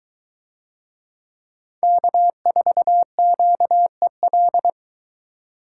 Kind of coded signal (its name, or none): Morse code